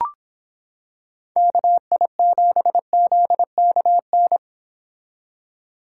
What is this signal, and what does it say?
Morse code, heard as a keyed tone, about 25 wpm, 700 Hz. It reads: KI7ZKN